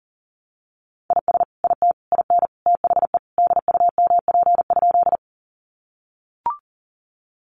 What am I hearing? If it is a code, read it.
Morse code, heard as a keyed tone, about 40 wpm, 700 Hz. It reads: IS IT IN THE DUMP?
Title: Sentences from Top 1300 Words 40wpm